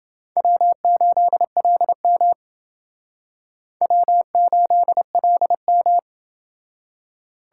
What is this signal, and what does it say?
Morse code, heard as a keyed tone, about 29 wpm, 700 Hz. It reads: W8LM W8LM